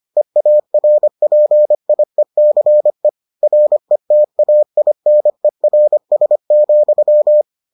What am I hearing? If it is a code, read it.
Morse code, heard as a keyed tone, about 25 wpm, 600 Hz. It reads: EARPIECE RETAINERS,